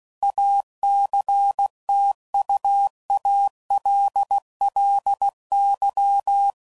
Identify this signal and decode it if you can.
Morse code, heard as a keyed tone, about 16 wpm, 800 Hz. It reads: ACTUALLY
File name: CW Code Groups @15WPM / QSOs / kc7haa